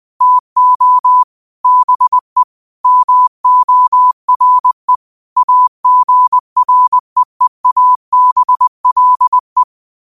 Morse code, heard as a keyed tone, about 20 wpm, 1000 Hz. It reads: TO BE MORE AGREEABLE
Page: New quotes every day in morse code at 20 Words per minute.